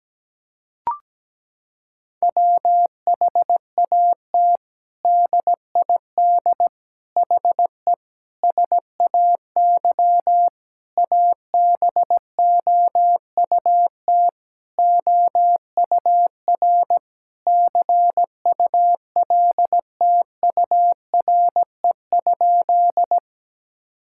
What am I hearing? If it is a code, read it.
Morse code, heard as a keyed tone, about 17 wpm, 700 Hz. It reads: WHAT DID HE SAY ABOUT OUR CULTURE?
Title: Sentences from Top 1000 Words - Review Part 6 - 17wpm